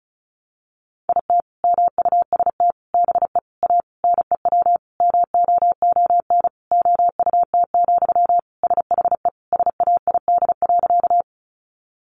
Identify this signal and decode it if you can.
Morse code, heard as a keyed tone, about 34 wpm, 700 Hz. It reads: IT MUST BE A NEW MOON OUT, SHE SAID.
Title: Sentences from Top 500 Words 35wpm